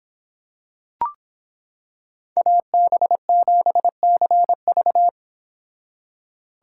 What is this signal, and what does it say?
Morse code, heard as a keyed tone, about 26 wpm, 700 Hz. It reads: AB7CV